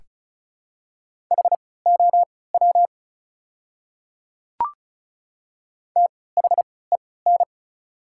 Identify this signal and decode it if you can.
Morse code, heard as a keyed tone, about 36 wpm, spaced as for 20 wpm, 700 Hz. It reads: HOW THEN
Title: 3x Character Spacing - Top 100 Words - 35wpm